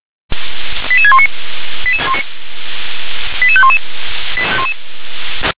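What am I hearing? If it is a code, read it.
Five-tone selective call (ZVEI), heard as quick runs of five tones, tones of about 70 ms. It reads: 88619 88511
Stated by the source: Five-tone ZVEI selective call, code 88152, received by SDR.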